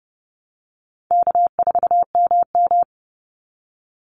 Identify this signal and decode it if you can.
Morse code, heard as a keyed tone, about 30 wpm, 700 Hz. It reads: K4MM
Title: Call Signs - 30wpm